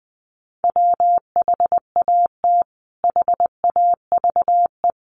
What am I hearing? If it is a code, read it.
Morse code, heard as a keyed tone, about 20 wpm, 700 Hz. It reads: WHAT HAVE
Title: Sentences from Top 500 Words 20wpm